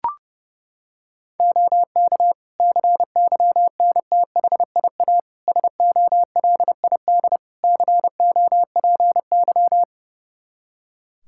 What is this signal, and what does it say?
Morse code, heard as a keyed tone, about 30 wpm, 700 Hz. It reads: OK CYNTHIA SOLID COPY
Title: K1USN SST QSO Practice 30wpm